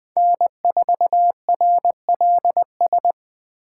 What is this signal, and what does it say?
Morse code, heard as a keyed tone, about 20 wpm, 700 Hz. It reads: N4RLS
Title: Speed Racing - USA 1x3 Call Signs 17wpm